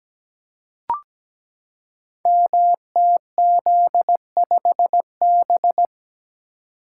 Morse code, heard as a keyed tone, about 17 wpm, 700 Hz. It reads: MTZ5B
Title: Code Groups - 5 Letter:Numbers - 17wpm